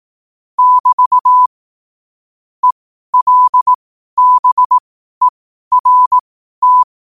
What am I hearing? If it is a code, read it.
Morse code, heard as a keyed tone, about 18 wpm, spaced as for 12 wpm, 1000 Hz. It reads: <BT> ELBERT
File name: qotdfiles / 12wpm